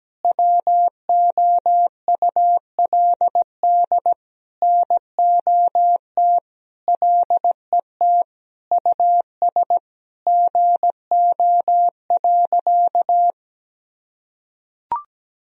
Morse code, heard as a keyed tone, about 17 wpm, 700 Hz. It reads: WOULD NOT LET US GO.
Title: Sentences from Top 1000 Words - Review Part 2 - 17wpm